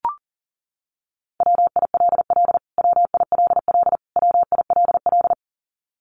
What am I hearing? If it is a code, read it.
Morse code, heard as a keyed tone, about 39 wpm, 700 Hz. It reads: WILL WILL WILL
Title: Top 200 Words Repeated 40wpm